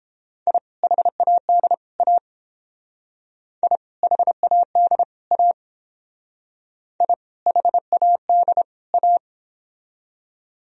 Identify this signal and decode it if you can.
Morse code, heard as a keyed tone, about 29 wpm, 700 Hz. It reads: I HAD A I HAD A I HAD A